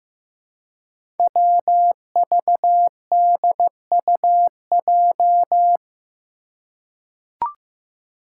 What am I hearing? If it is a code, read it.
Morse code, heard as a keyed tone, about 15 wpm, 700 Hz. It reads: WVDUJ
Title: Five Challenging Letter-Number Combinations 15wpm